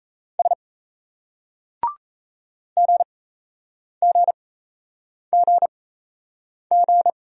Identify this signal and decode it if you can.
Morse code, heard as a keyed tone, about 38 wpm, spaced as for 8 wpm, 700 Hz. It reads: S GGGG